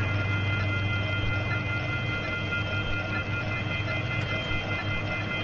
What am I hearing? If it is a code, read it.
Off-hook warning tone, 1400 + 2060 + 2450 + 2600 Hz, with noise.
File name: SaturdayMorningSerial DialingIn free103point9 WGXC